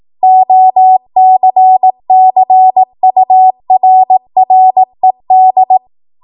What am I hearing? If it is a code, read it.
Morse code, heard as a keyed tone, about 18 wpm, 750 Hz. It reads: OCCURRED
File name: Morse / 18WPM